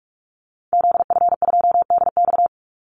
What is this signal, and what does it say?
Morse code, heard as a keyed tone, about 44 wpm, 700 Hz. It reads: ZF2DX
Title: Call Signs - 45wpm